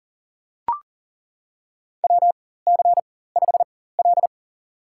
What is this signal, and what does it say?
Morse code, heard as a keyed tone, about 40 wpm, spaced as for 19 wpm, 700 Hz. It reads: WC5L
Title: USA Call Signs - 4 Characters - 4x Character Spacing - 40wpm